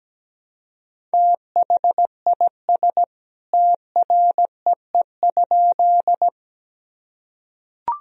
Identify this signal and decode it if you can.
Morse code, heard as a keyed tone, about 17 wpm, 700 Hz. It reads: THIS TREE?
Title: Sentences from Top 1000 Words - Review Part 1 - 17wpm